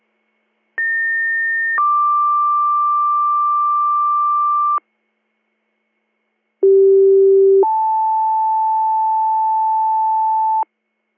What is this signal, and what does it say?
Two-tone page, heard as a one-second tone, then a three-second tone, about 1800 then 1180 Hz, about 390 then 880 Hz. It is heard from 0.8 to 10.6 s.